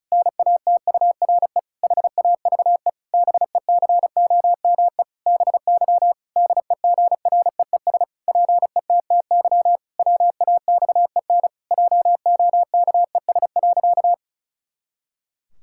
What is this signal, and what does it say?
Morse code, heard as a keyed tone, about 36 wpm, 700 Hz. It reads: NATURE HAVE BECOME BY DEGREES PETTY WAXEN JOKES.
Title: Pangram Challenge v2 - 35wpm